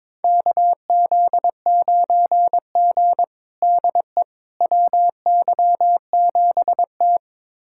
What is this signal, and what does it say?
Morse code, heard as a keyed tone, about 22 wpm, 700 Hz. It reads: KZ9G DE WY7T
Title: USA Callsign DE Callsign - 4 Characters - 22wpm